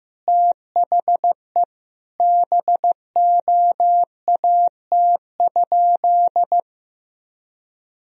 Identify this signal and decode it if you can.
Morse code, heard as a keyed tone, about 15 wpm, 700 Hz. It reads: THE BOAT?
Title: Sentences from Top 1000 Words - Review Part 4 - 15wpm